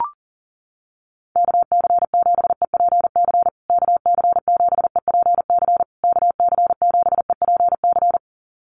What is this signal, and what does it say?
Morse code, heard as a keyed tone, about 40 wpm, 700 Hz. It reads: KC7EPC KC7EPC KC7EPC
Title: USA Call Signs Repeated 3x 40wpm